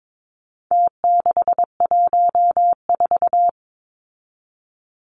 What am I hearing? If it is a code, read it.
Morse code, heard as a keyed tone, about 22 wpm, 700 Hz. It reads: T614
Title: Numbers - Contest Style - 4-Digit 22wpm